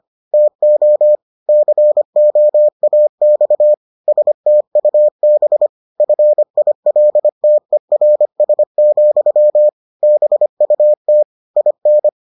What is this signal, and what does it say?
Morse code, heard as a keyed tone, about 25 wpm, 600 Hz. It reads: TO COAX STUB FILTERS, BUT IN